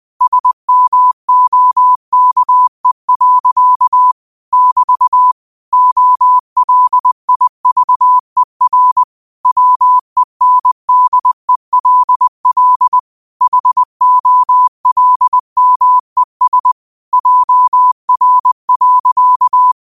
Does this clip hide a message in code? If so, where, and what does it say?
Yes: Morse code from 0.2 to 19.7 s, heard as a keyed tone, about 20 wpm, 1000 Hz. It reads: SMOKE. <BT> OLIVER WENDELL HOLMES JR.